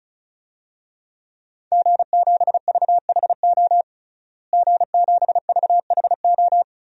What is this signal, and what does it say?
Morse code, heard as a keyed tone, about 36 wpm, 700 Hz. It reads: G7VHO G7VHO
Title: DXCC-100 Call Signs - 5 Characters - Repeated 2x - 35wpm